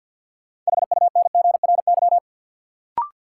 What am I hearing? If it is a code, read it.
Morse code, heard as a keyed tone, about 53 wpm, 700 Hz. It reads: HUNGRY
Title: Sentences from Top 1400 Words - New Word Drill Alt - 50wpm